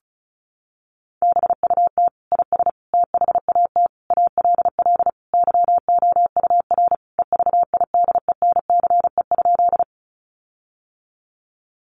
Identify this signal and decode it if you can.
Morse code, heard as a keyed tone, about 34 wpm, 700 Hz. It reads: BUT IS THAT ALL YOUR EVIDENCE?
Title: Sentences from Top 1300 Words 35wpm